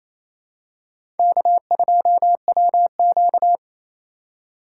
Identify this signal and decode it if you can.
Morse code, heard as a keyed tone, about 28 wpm, 700 Hz. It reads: K2WQ
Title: USA Call Signs - 2x Character Spacing - 28wpm